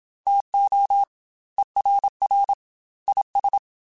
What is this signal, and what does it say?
Morse code, heard as a keyed tone, about 27 wpm, 800 Hz. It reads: TO ERR IS
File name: Chuckles / Chuckles-mp3s / HumanErr @25